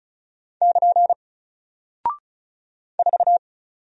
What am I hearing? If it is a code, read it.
Morse code, heard as a keyed tone, about 36 wpm, 700 Hz. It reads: <KN> 4
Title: Single Character (Letter:Number:Punctuation:Prosign) Alt 35wpm